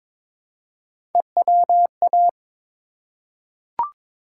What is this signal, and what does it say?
Morse code, heard as a keyed tone, about 22 wpm, 700 Hz. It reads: EWA
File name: ARRL Sections v2 - 22wpm